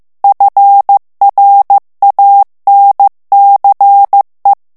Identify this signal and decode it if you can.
Morse code, heard as a keyed tone, about 15 wpm, 800 Hz. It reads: FRANCE
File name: morse mp3 / FRANCAIS TEXTE CLAIR 1 VITESSE 14 MOTS MINUTES